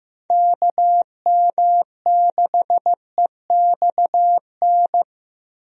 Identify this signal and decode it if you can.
Morse code, heard as a keyed tone, about 15 wpm, 700 Hz. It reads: KM6EXN
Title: USA Call Signs - 6 Characters - Repeated 2x - 15wpm